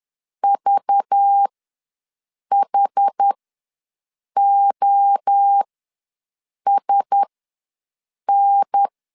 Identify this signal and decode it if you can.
Morse code, heard as a keyed tone, about 10 wpm, spaced as for 6 wpm, 800 Hz. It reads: VHOSN
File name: LEZIONI / 5 LBVC / 5 LBV ws 10wpm 3645